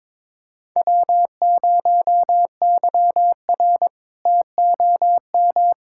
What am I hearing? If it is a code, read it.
Morse code, heard as a keyed tone, about 21 wpm, 700 Hz. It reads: W0YR TOM
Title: K1USN SST QSO Practice - 22wpm